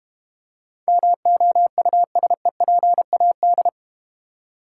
Morse code, heard as a keyed tone, about 31 wpm, 700 Hz. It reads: MOUSEPAD